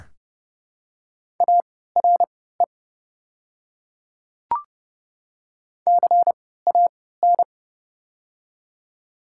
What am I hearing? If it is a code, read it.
Morse code, heard as a keyed tone, about 30 wpm, spaced as for 17 wpm, 700 Hz. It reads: ARE CAN
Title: 3x Character Spacing - Top 500 Words - 30wpm